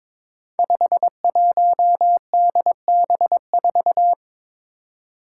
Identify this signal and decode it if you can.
Morse code, heard as a keyed tone, about 23 wpm, 700 Hz. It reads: 51DB4
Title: Five Challenging Letter-Number Combinations 22wpm